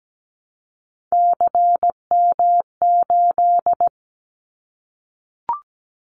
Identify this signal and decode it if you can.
Morse code, heard as a keyed tone, about 17 wpm, 700 Hz. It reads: CM8
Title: Callsign Prefixes for Top 100 DXCC - 17wpm